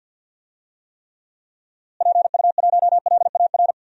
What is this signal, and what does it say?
Morse code, heard as a keyed tone, about 51 wpm, 700 Hz. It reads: PU1LAR
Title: DXCC-100 Call Signs - 6 Characters - Repeated 3x - 50wpm